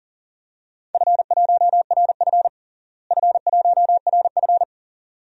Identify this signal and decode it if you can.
Morse code, heard as a keyed tone, about 40 wpm, 700 Hz. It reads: F1RF F1RF
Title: DXCC-100 Call Signs - 4 Characters - Repeated 2x - 40wpm